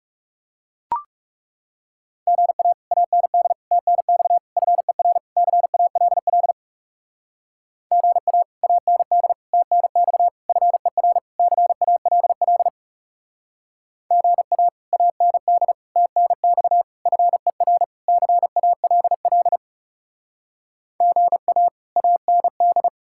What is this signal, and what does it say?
Morse code, heard as a keyed tone, about 35 wpm, 700 Hz. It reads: GA AND TNX FER CALL GA AND TNX FER CALL GA AND TNX FER CALL GA AND
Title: Speed-Racing Common QSO Phrases Encore - 30wpm